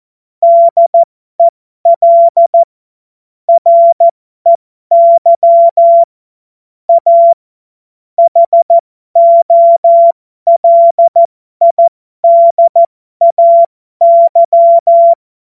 Morse code, heard as a keyed tone, about 14 wpm, spaced as for 12 wpm, 680 Hz. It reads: DEL REY A HOLIDAY
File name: Morsekurs / Witze